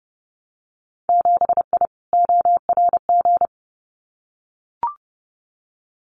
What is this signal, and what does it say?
Morse code, heard as a keyed tone, about 30 wpm, 700 Hz. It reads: 7I ORG
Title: Winter Field Day Exchanges Encore 30wpm